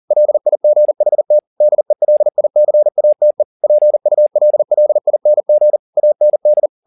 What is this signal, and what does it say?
Morse code, heard as a keyed tone, about 40 wpm, 600 Hz. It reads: LIGHT DELICATE PULLING AND